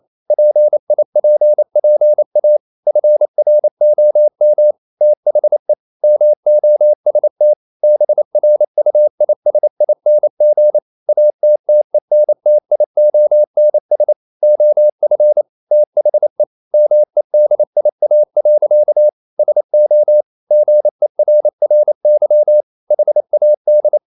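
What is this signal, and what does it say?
Morse code, heard as a keyed tone, about 28 wpm, 600 Hz. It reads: PIPPA FROM THE MOST BRUISING ATTENTIONS OF THE MEDIA. SO GERRY HAD
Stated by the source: Morse code practice at 28 wpm, text of Ken Follett's novel Never in English, from CW med Gnister.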